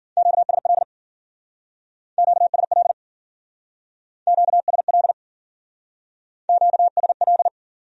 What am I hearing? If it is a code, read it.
Morse code, heard as a keyed tone, about 52 wpm, 700 Hz. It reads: QSL QSL QSL QSL